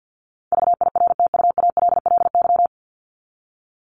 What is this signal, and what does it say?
Morse code, heard as a keyed tone, about 50 wpm, 700 Hz. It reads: VIRTUALLY
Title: LY Suffix - Top 100 Words with Base Words 50wpm